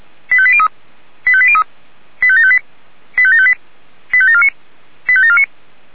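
Five-tone selective call (ZVEI), heard as quick runs of five tones, tones of about 70 ms. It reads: 85803 85803 85758 85758 85749 85749
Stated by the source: ZVEI five-tone call received on SDR.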